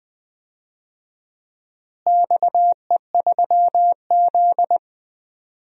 Morse code, heard as a keyed tone, about 20 wpm, 700 Hz. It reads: XE3Z
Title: DXCC-100 Call Signs - 4 Characters - Repeated 2x - 20wpm